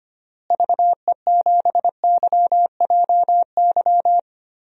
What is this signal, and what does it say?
Morse code, heard as a keyed tone, about 25 wpm, 700 Hz. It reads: VE7YJY
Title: DXCC-100 Call Signs - 6 Characters - Repeated 2x - 25wpm